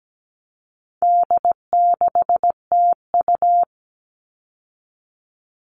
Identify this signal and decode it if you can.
Morse code, heard as a keyed tone, about 17 wpm, 700 Hz. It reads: D6TU
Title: All Cut Numbers - 4-Digit 17wpm